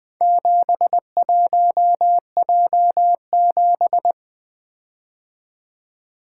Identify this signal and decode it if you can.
Morse code, heard as a keyed tone, about 20 wpm, 700 Hz. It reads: 71J7
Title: Four Letter-Number Combinations - 20wpm